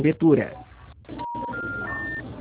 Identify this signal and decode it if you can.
Special information tone, with speech.
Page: Piracicaba - SP